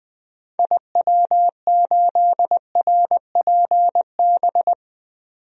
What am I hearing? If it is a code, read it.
Morse code, heard as a keyed tone, about 20 wpm, 700 Hz. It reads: IW8RPB
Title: DXCC-100 Call Signs - 6 Characters - 20wpm